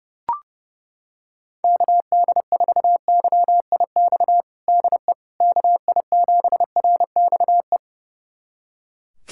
Morse code, heard as a keyed tone, about 30 wpm, 700 Hz. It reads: KD4YIX DE KI7RXE
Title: USA Callsign DE Callsign - 6 Characters - 30wpm